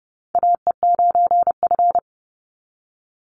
Morse code, heard as a keyed tone, about 30 wpm, 700 Hz. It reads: AE9F
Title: NA SOTA Chaser Callsigns 30wpm